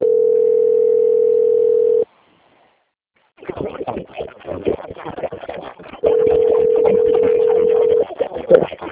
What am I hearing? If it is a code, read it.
Ringback tone (North America), 440 + 480 Hz, with speech.
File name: out- - - - - .31386